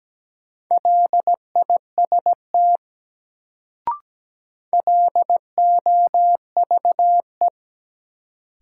Morse code, heard as a keyed tone, about 17 wpm, 700 Hz. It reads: LIST LOVE
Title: Sentences from Top 400 Words - New Word Drill - 17wpm